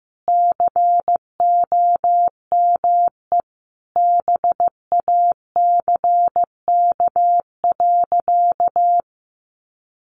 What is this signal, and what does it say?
Morse code, heard as a keyed tone, about 15 wpm, 700 Hz. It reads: COME BACK.
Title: A Sound Of Thunder 15wpm